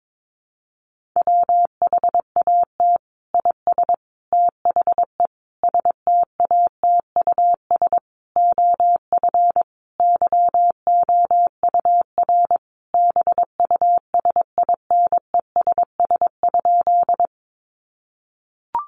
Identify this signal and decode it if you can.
Morse code, heard as a keyed tone, about 22 wpm, 700 Hz. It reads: WHAT IS THE STATUS OF YOUR BUSINESS?
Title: Sentences from Top 1400 Words 22wpm